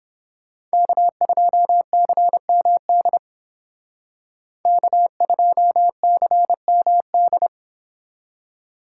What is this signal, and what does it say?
Morse code, heard as a keyed tone, about 31 wpm, 700 Hz. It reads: K2CMD K2CMD